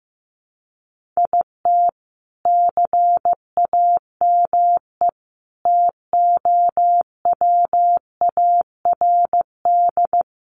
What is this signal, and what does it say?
Morse code, heard as a keyed tone, about 15 wpm, 700 Hz. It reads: IT CAME TOWARD
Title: Sentences from Top 1500 Words 15wpm